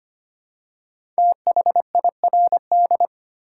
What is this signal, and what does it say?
Morse code, heard as a keyed tone, about 25 wpm, 700 Hz. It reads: THIRD
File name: 5 Letter Words - First 3 Letters Given - 25wpm